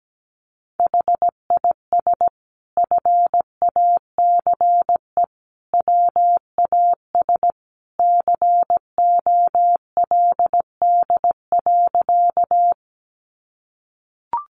Morse code, heard as a keyed tone, about 17 wpm, 700 Hz. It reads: HIS FACE WAS COLD.
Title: A Sound Of Thunder 17wpm